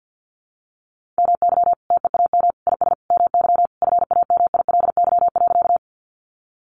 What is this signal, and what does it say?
Morse code, heard as a keyed tone, about 50 wpm, 700 Hz. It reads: MY TEAM IS MY FAMILY.
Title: Sentences from Top 1500 Words 50wpm